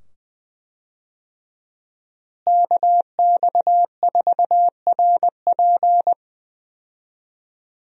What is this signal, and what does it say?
Morse code, heard as a keyed tone, about 20 wpm, 700 Hz. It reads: KX4RP
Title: USA Call Signs - 5 Characters - Repeated 2x - 20wpm